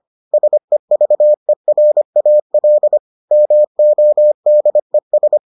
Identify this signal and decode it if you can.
Morse code, heard as a keyed tone, about 25 wpm, 600 Hz. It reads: SEVERAL MODES